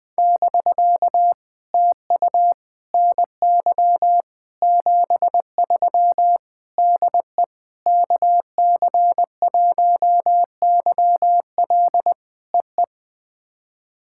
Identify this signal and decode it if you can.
Morse code, heard as a keyed tone, about 19 wpm, 700 Hz. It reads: <BK> TU NY 73 DE KC1YL EE
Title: POTA QSO Practice - 20wpm